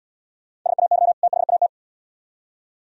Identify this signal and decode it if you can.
Morse code, heard as a keyed tone, about 68 wpm, 700 Hz. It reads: SAY THAT